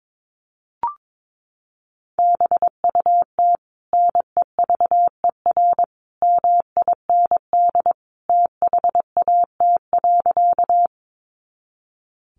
Morse code, heard as a keyed tone, about 22 wpm, 700 Hz. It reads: BUT NEVER MIND THAT.
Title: Sentences from Top 500 Words 22wpm